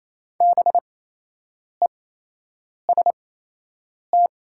Morse code, heard as a keyed tone, about 28 wpm, spaced as for 8 wpm, 690 Hz. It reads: BEST
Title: 8x Character Spacing - Top 500 Words - 28wpm